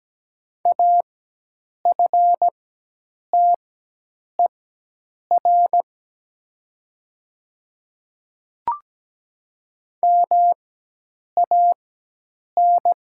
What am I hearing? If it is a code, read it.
Morse code, heard as a keyed tone, about 17 wpm, spaced as for 8 wpm, 700 Hz. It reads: AFTER MAN
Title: 4x Character Spacing - Top 300 Words - 17wpm